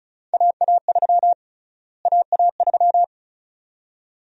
Morse code, heard as a keyed tone, about 36 wpm, 700 Hz. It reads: AA3 AA3
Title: USA Call Sign Prefixes - Repeated 2x - 35wpm